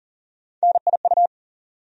Morse code, heard as a keyed tone, about 40 wpm, 700 Hz. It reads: NIU
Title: USA Call Sign Postfixes - 3 Letters 40wpm